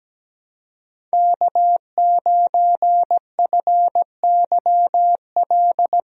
Morse code, heard as a keyed tone, about 17 wpm, 700 Hz. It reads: K9FYL
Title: USA Call Signs - 5 Characters - Repeated 2x - 17wpm